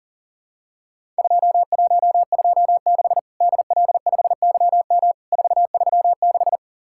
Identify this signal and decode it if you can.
Morse code, heard as a keyed tone, about 40 wpm, 700 Hz. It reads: 2126 DL5YM 436
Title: Worked All Europe DX Contest - QTC Exchange - 40wpm